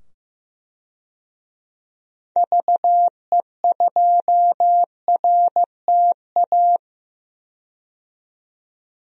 Morse code, heard as a keyed tone, about 16 wpm, 700 Hz. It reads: VE2RTA
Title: DXCC-100 Call Signs - 6 Characters - 15wpm